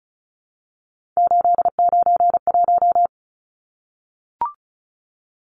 Morse code, heard as a keyed tone, about 34 wpm, 700 Hz. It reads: 891
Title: Numbers - 3-Digit 35wpm